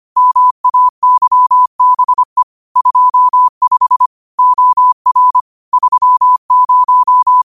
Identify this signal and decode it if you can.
Morse code, heard as a keyed tone, about 25 wpm, 1000 Hz. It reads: MAYBE 25 OR 30
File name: qotdfiles / 25wpm